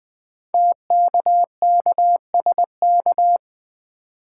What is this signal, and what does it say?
Morse code, heard as a keyed tone, about 20 wpm, 700 Hz. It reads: TKKSK
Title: Code Groups - 5 Letters - 20wpm